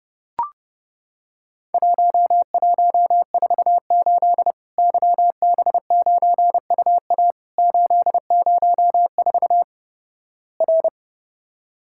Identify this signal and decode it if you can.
Morse code, heard as a keyed tone, about 32 wpm, 700 Hz. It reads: 1148 YB9UA 804 R
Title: Worked All Europe DX Contest - QTC Exchange - 30wpm